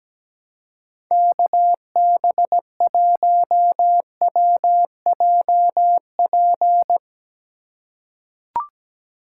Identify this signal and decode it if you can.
Morse code, heard as a keyed tone, about 17 wpm, 700 Hz. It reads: KB1WJP
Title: USA Call Signs - 6 Characters - 3x Character Spacing - 17wpm